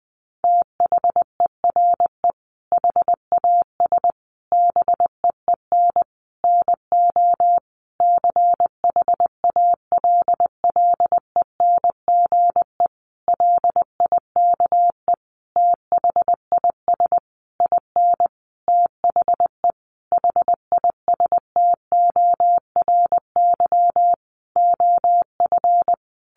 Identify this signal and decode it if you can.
Morse code, heard as a keyed tone, about 20 wpm, 700 Hz. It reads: THERE HAS BEEN NO CHALLENGE LIKE THIS IN THE HISTORY OF
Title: Sentences from Top 1500 Words 20wpm